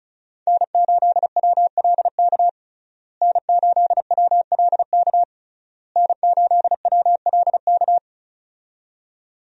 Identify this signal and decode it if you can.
Morse code, heard as a keyed tone, about 34 wpm, 700 Hz. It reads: N8WLK N8WLK N8WLK
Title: USA Call Signs - 5 Characters - Repeated 3x - 35wpm